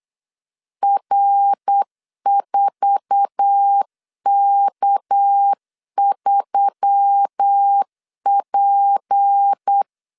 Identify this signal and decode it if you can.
Morse code, heard as a keyed tone, about 8 wpm, 800 Hz. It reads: R4K3P